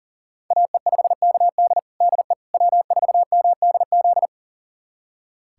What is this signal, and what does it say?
Morse code, heard as a keyed tone, about 40 wpm, 700 Hz. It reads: AE5KD DE W4MDZ
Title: USA Callsign DE Callsign - 5 Characters - 40wpm